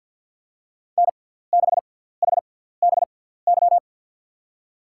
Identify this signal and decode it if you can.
Morse code, heard as a keyed tone, about 53 wpm, spaced as for 17 wpm, 700 Hz. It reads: N6HBY
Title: USA Call Signs - 5 Characters - 6x Character Spacing - 50wpm